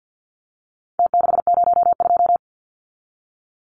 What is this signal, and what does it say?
Morse code, heard as a keyed tone, about 48 wpm, 700 Hz. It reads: T602
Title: Numbers - Contest Style - 4-Digit 50wpm